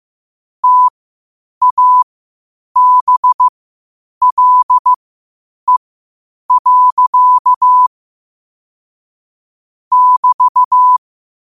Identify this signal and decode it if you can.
Morse code, heard as a keyed tone, about 15 wpm, spaced as for 8 wpm, 1000 Hz. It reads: TABLE. <BT>
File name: qotdfiles / 8wpm